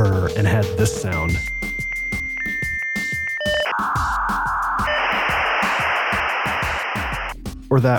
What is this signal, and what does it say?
Dial-up modem handshake, with speech.